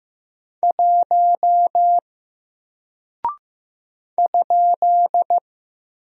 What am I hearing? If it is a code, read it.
Morse code, heard as a keyed tone, about 15 wpm, 700 Hz. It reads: 1 ?
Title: Single Character (Letter:Number:Punctuation:Prosign) 15wpm